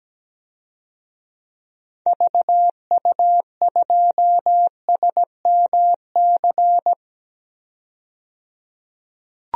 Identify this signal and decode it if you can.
Morse code, heard as a keyed tone, about 17 wpm, 700 Hz. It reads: VU2SMC
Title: DXCC-100 Call Signs - 6 Characters - 17wpm